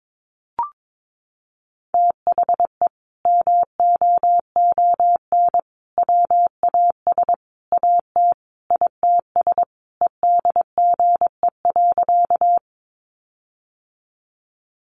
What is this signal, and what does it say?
Morse code, heard as a keyed tone, about 22 wpm, 700 Hz. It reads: THE MOON WAS AT ITS EDGE.
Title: Sentences from Top 1300 Words 22wpm